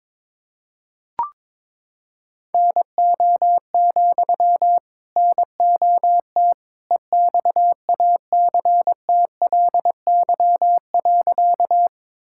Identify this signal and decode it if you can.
Morse code, heard as a keyed tone, about 22 wpm, 700 Hz. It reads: NO, NOT EXACTLY.
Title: Sentences from Top 1000 Words - Review Part 6 - 22wpm